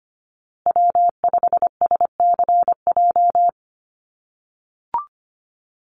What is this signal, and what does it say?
Morse code, heard as a keyed tone, about 25 wpm, 700 Hz. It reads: W5SCJ
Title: USA Call Signs Repeated 2x 25wpm